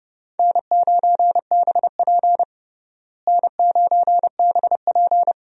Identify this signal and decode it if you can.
Morse code, heard as a keyed tone, about 30 wpm, 700 Hz. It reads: N9BP N9BP
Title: USA Call Signs - 4 Characters - Repeated 2x - 30wpm